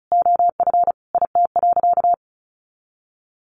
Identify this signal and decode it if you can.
Morse code, heard as a keyed tone, about 34 wpm, 700 Hz. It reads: OF IT.
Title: Sentences from Top 1200 Words 35wpm